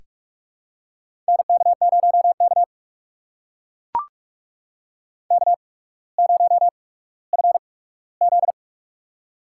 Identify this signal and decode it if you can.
Morse code, heard as a keyed tone, about 43 wpm, 690 Hz. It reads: N K 0 K K 0 F Z
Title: USA Call Signs - 8x Character Spacing - 45wpm